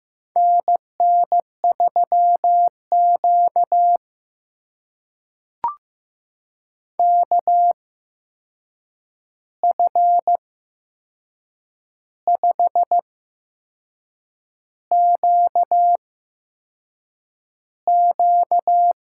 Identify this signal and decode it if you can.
Morse code, heard as a keyed tone, about 15 wpm, spaced as for 6 wpm, 700 Hz. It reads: NN3Q KF5QQ
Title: USA Call Signs - 8x Character Spacing - 15wpm